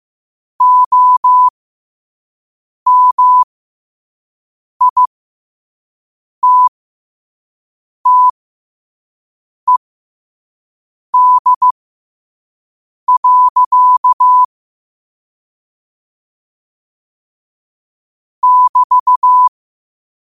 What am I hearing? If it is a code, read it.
Morse code, heard as a keyed tone, about 15 wpm, spaced as for 5 wpm, 1000 Hz. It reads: OMITTED. <BT>
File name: qotdfiles / 5wpm